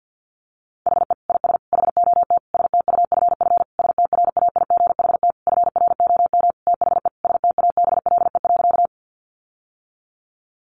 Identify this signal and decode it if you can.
Morse code, heard as a keyed tone, about 50 wpm, 700 Hz. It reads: HE IS HOT STUFF STRAIGHT FROM THE STABLE.
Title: Sentences from Top 1500 Words 50wpm